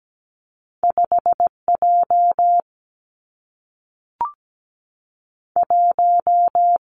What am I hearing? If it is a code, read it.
Morse code, heard as a keyed tone, about 17 wpm, 700 Hz. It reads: 5 J 1
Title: Two Challenging Letter-Number Combinations 17wpm